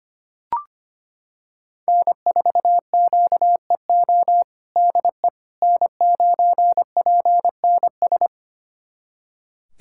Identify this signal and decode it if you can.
Morse code, heard as a keyed tone, about 25 wpm, 700 Hz. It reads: N4QEO DE N9PNS
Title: USA Callsign DE Callsign - 5 Characters - 25wpm